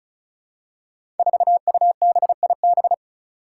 Morse code, heard as a keyed tone, about 36 wpm, 700 Hz. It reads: 4UBIB
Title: Five Challenging Letter-Number Combinations 35wpm